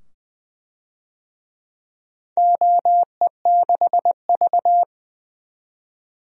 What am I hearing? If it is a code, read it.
Morse code, heard as a keyed tone, about 20 wpm, 700 Hz. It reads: OE6V
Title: DXCC-100 Call Signs - 4 Characters - 20wpm